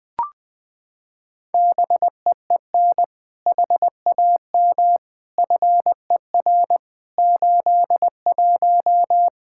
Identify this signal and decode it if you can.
Morse code, heard as a keyed tone, about 20 wpm, 700 Hz. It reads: BEEN HAM FER 81
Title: Practice QSO - Exchange 3 - More Info - 20wpm